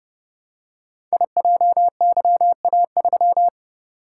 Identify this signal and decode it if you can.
Morse code, heard as a keyed tone, about 30 wpm, 700 Hz. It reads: IJYA3
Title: Code Groups - 5 Letter:Numbers - 30wpm